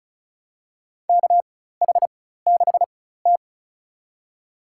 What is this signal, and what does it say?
Morse code, heard as a keyed tone, about 36 wpm, spaced as for 16 wpm, 700 Hz. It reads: KH6T
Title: USA Call Signs - 4 Characters - 4x Character Spacing - 35wpm